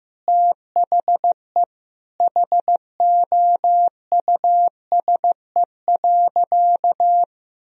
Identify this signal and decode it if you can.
Morse code, heard as a keyed tone, about 15 wpm, 700 Hz. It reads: THE HOUSE.
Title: Sentences from Top 1000 Words - Review Part 2 - 15wpm